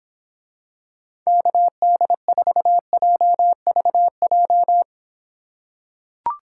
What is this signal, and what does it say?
Morse code, heard as a keyed tone, about 26 wpm, 700 Hz. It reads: KD4JVJ